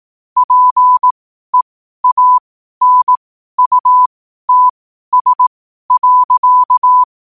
Morse code, heard as a keyed tone, about 18 wpm, spaced as for 12 wpm, 1000 Hz. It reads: PEANUTS.
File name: qotdfiles / 12wpm